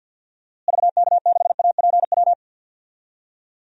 Morse code, heard as a keyed tone, about 51 wpm, 700 Hz. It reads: VK6APW
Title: DXCC-100 Call Signs - 6 Characters - Repeated 3x - 50wpm